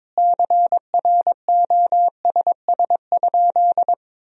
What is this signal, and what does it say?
Morse code, heard as a keyed tone, about 22 wpm, 700 Hz. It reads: CROSS?
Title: 3x Word-Spacing - Sentences from Top 300 Words - 22wpm